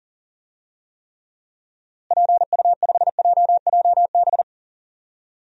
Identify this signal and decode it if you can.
Morse code, heard as a keyed tone, about 37 wpm, 700 Hz. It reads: PU5JJB
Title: DXCC-100 Call Signs - 6 Characters - Repeated 3x - 40wpm